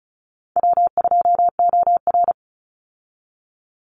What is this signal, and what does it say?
Morse code, heard as a keyed tone, about 34 wpm, 700 Hz. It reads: W2OR
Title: Easy Call Signs Repeated 3x - 35wpm